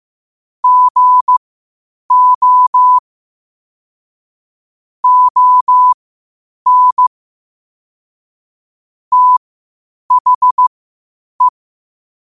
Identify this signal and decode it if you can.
Morse code, heard as a keyed tone, about 15 wpm, spaced as for 8 wpm, 1000 Hz. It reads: GO ON THE
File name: qotdfiles / 8wpm